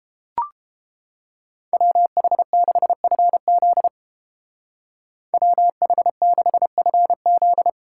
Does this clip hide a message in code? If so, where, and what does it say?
Yes: Morse code from 1.7 to 7.7 s, heard as a keyed tone, about 29 wpm, 700 Hz. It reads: WH6FZ WH6FZ